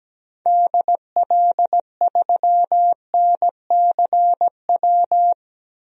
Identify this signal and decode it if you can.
Morse code, heard as a keyed tone, about 17 wpm, 700 Hz. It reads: DL3NCW
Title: DXCC-100 Call Signs - 6 Characters - 17wpm